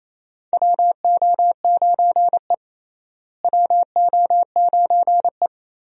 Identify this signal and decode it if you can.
Morse code, heard as a keyed tone, about 28 wpm, 700 Hz. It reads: WO9E WO9E
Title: USA Call Signs - 4 Characters - Repeated 2x - 28wpm